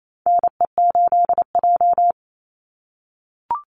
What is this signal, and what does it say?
Morse code, heard as a keyed tone, about 28 wpm, 700 Hz. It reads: NE8J
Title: Call Signs - 28wpm